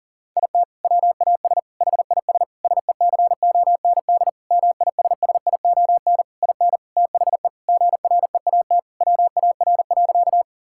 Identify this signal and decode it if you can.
Morse code, heard as a keyed tone, about 37 wpm, 700 Hz. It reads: IT WAS HIS SECOND MISSION IN THE GREAT WAR.
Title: Sentences from Top 1600 Words 40wpm